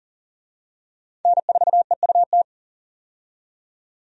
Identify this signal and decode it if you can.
Morse code, heard as a keyed tone, about 40 wpm, 700 Hz. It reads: N4EUT
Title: USA Call Signs - 5 Characters - 40wpm